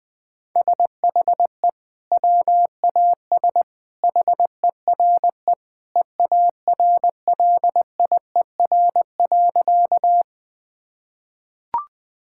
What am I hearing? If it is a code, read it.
Morse code, heard as a keyed tone, about 20 wpm, 700 Hz. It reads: SHE WAS HERE EARLIER.
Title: Sentences with 2-Word Phrases - 20wpm